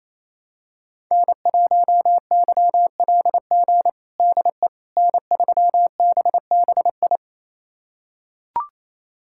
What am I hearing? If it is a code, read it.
Morse code, heard as a keyed tone, about 28 wpm, 700 Hz. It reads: N1YLG DE N3BBI
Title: USA Callsign DE Callsign - 5 Characters - 28wpm